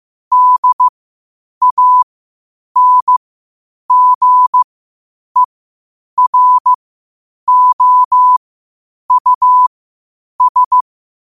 Morse code, heard as a keyed tone, about 15 wpm, spaced as for 8 wpm, 1000 Hz. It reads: DANGEROUS
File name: qotdfiles / 8wpm